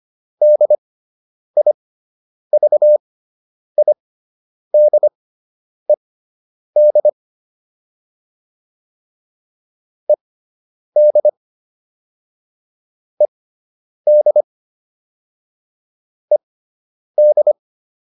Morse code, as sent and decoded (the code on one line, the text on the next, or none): -.. .. ...- .. -.. -.. / . -.. / . -.. / . -..
DIVIDED ED ED ED